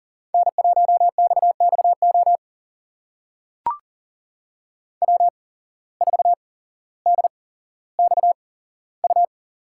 Morse code, heard as a keyed tone, about 42 wpm, spaced as for 17 wpm, 700 Hz. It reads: N1XXO W4DXU
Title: USA Call Signs - 5 Characters - 8x Character Spacing - 40wpm